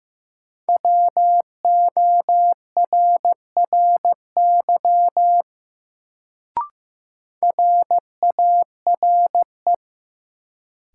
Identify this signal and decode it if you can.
Morse code, heard as a keyed tone, about 15 wpm, 700 Hz. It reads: WORRY RARE
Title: Sentences from Top 1100 Words - New Word Drill - 15wpm